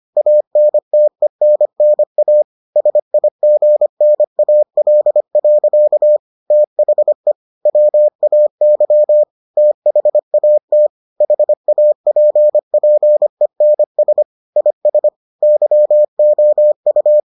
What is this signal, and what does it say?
Morse code, heard as a keyed tone, about 25 wpm, 600 Hz. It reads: ANTENNA SIGNAL. THE WAY THAT HAPPENS IS YOU